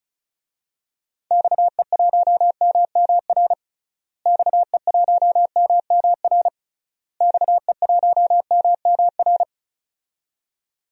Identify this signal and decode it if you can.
Morse code, heard as a keyed tone, about 36 wpm, 700 Hz. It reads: XE1MMR XE1MMR XE1MMR
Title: DXCC-100 Call Signs - 6 Characters - Repeated 3x - 35wpm